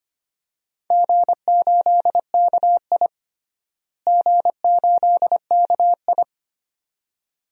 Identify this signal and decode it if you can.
Morse code, heard as a keyed tone, about 25 wpm, 700 Hz. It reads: G8KI G8KI
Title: DXCC-100 Call Signs - 4 Characters - Repeated 2x - 25wpm